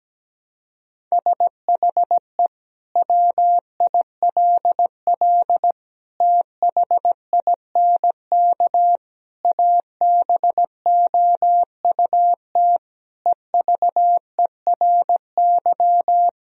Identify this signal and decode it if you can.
Morse code, heard as a keyed tone, about 17 wpm, 700 Hz. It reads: SHE WILL THINK ABOUT EVERY
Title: Sentences from Top 1000 Words - Review Part 1 - 17wpm